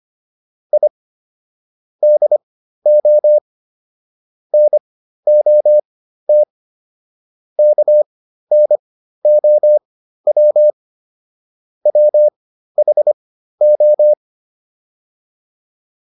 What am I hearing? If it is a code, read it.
Morse code, heard as a keyed tone, about 25 wpm, spaced as for 13 wpm, 600 Hz. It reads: I DO NOT KNOW WHO